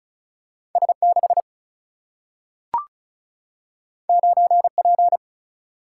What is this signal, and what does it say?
Morse code, heard as a keyed tone, about 36 wpm, 700 Hz. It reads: S6 9P
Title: Single-Letter Single-Number 35wpm